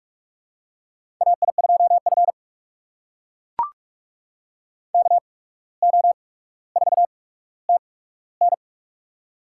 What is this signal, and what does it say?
Morse code, heard as a keyed tone, about 44 wpm, spaced as for 18 wpm, 700 Hz. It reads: AI2F KO4TN
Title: USA Call Signs - 8x Character Spacing - 45wpm